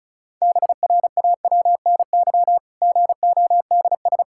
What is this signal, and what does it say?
Morse code, heard as a keyed tone, about 36 wpm, 700 Hz. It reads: BRAWNY GODS
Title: Pangram Challenge v2 - 35wpm